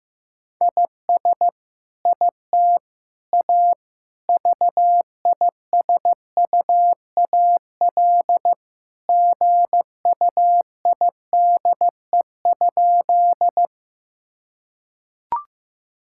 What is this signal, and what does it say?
Morse code, heard as a keyed tone, about 16 wpm, 700 Hz. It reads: IS IT A VISUAL GUIDE?
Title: Sentences from Top 1000 Words - Review Part 6 - 15wpm